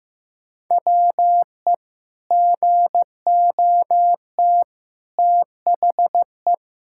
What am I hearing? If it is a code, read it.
Morse code, heard as a keyed tone, about 15 wpm, 700 Hz. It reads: WE GOT THE
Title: Sentences from Top 1000 Words - Review Part 5 - 15wpm